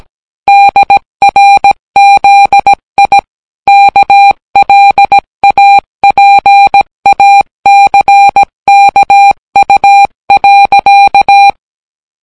Morse code, heard as a keyed tone, about 18 wpm, 800 Hz. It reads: DRZI KLAPACKU.